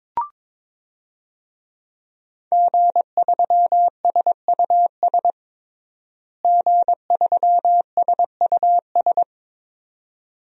Morse code, heard as a keyed tone, about 22 wpm, 700 Hz. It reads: G3SUS G3SUS
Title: DXCC-100 Call Signs - 5 Characters - Repeated 2x - 22wpm